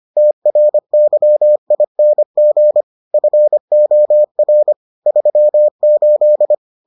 Morse code, heard as a keyed tone, about 25 wpm, 600 Hz. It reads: TRYING FOR 38